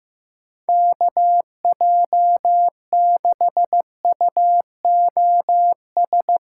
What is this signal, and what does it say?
Morse code, heard as a keyed tone, about 16 wpm, 700 Hz. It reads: KJ6UOS
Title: USA Callsign DE Callsign - 6 Characters - 15wpm